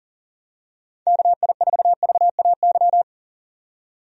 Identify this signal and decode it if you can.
Morse code, heard as a keyed tone, about 40 wpm, 700 Hz. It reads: KI4VAY
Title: USA Call Signs - 6 Characters - 40wpm